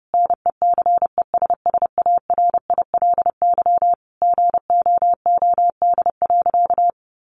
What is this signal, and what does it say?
Morse code, heard as a keyed tone, about 30 wpm, 700 Hz. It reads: NECESSARILY GOOD.
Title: Sentences from Top 1400 Words 30wpm